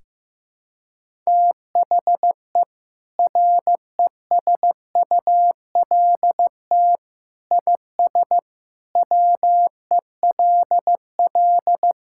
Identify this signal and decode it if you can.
Morse code, heard as a keyed tone, about 15 wpm, 700 Hz. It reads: THE RESULT IS WELL
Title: Sentences from Top 1000 Words - Review Part 5 - 15wpm